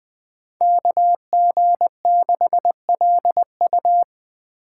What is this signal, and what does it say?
Morse code, heard as a keyed tone, about 20 wpm, 700 Hz. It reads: KG6LU
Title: USA Call Signs - 5 Characters - 2x Character Spacing - 20wpm